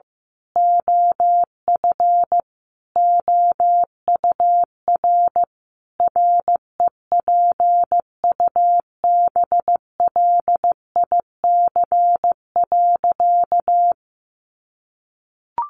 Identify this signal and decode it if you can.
Morse code, heard as a keyed tone, about 15 wpm, 700 Hz. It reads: OF OUR REPUBLIC.
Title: Sentences from Top 1500 Words 15wpm